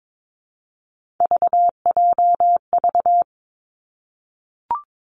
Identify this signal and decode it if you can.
Morse code, heard as a keyed tone, about 22 wpm, 700 Hz. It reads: VJV
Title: Three Challenging Letter-Number Combinations 22wpm